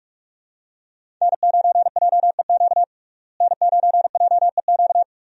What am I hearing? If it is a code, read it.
Morse code, heard as a keyed tone, about 46 wpm, 700 Hz. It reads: N9JEQ N9JEQ
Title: USA Call Signs - 5 Characters - Repeated 3x - 45wpm